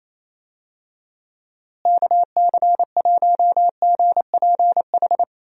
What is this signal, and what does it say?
Morse code, heard as a keyed tone, about 28 wpm, 700 Hz. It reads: KC1GPH
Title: USA Call Signs - 6 Characters - Repeated 3x - 28wpm